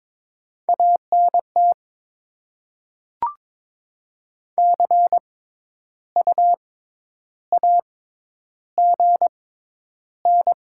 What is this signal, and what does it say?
Morse code, heard as a keyed tone, about 22 wpm, spaced as for 10 wpm, 690 Hz. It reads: ANT CU AGN
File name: Top 20 QSO Elements - 6x Character Spacing - 22wpm